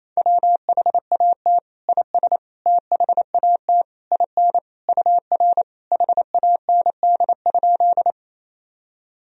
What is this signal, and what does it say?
Morse code, heard as a keyed tone, about 28 wpm, 700 Hz. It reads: WHAT IS THAT IN UR HAND?
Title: Sentences from Top 200 Words and Abbreviations 28wpm